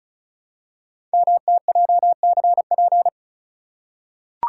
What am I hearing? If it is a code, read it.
Morse code, heard as a keyed tone, about 35 wpm, 700 Hz. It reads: MTJCP
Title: Code Groups - 5 Letters - 35wpm